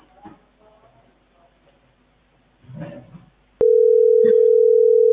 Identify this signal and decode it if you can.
Ringback tone (North America), 440 + 480 Hz.